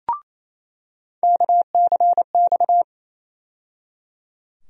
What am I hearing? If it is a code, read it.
Morse code, heard as a keyed tone, about 28 wpm, 700 Hz. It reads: KCX
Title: USA Call Sign Postfixes - 3 Letters 28wpm